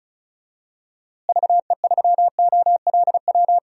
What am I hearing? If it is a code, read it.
Morse code, heard as a keyed tone, about 36 wpm, 700 Hz. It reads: VE3OLW
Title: DXCC-100 Call Signs - 6 Characters - Repeated 2x - 35wpm